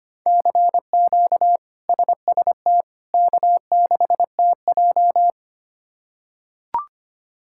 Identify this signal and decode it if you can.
Morse code, heard as a keyed tone, about 25 wpm, 700 Hz. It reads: CQ SST K6TJ
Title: K1USN SST QSO Practice - 25wpm